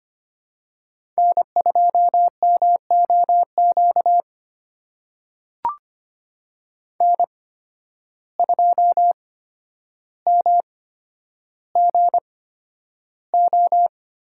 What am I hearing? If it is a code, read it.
Morse code, heard as a keyed tone, about 25 wpm, spaced as for 11 wpm, 700 Hz. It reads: N2MOQ N2MGO
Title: USA Call Signs - 5 Characters - 8x Character Spacing - 25wpm